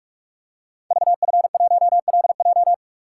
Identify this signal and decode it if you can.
Morse code, heard as a keyed tone, about 44 wpm, 700 Hz. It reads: VF1LJ
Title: Code Groups - 5 Letter:Numbers - 45wpm